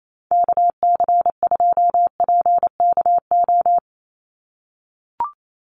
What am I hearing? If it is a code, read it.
Morse code, heard as a keyed tone, about 28 wpm, 700 Hz. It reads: KC2PKO
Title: USA Call Signs 28wpm